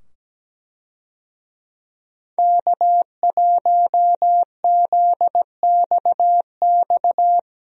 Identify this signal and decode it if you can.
Morse code, heard as a keyed tone, about 17 wpm, 700 Hz. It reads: K1ZXX
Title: USA Call Signs - 5 Characters - 17wpm